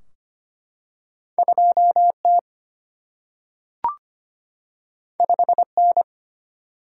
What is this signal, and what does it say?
Morse code, heard as a keyed tone, about 25 wpm, 700 Hz. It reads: 2T 5N
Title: Single-Letter Single-Number 25wpm